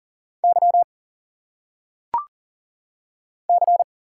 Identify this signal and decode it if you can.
Morse code, heard as a keyed tone, about 40 wpm, 700 Hz. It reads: Y C
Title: Single Letters - Phonetic Pronunciation 40wpm